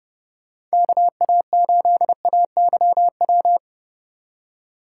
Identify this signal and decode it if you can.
Morse code, heard as a keyed tone, about 37 wpm, 700 Hz. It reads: KA8AYW
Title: Speed Racing - USA 2x3 Call Signs 30wpm